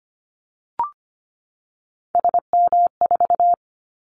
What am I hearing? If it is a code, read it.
Morse code, heard as a keyed tone, about 25 wpm, 700 Hz. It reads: SM4
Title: Callsign Prefixes for Top 25 DXCC - 25wpm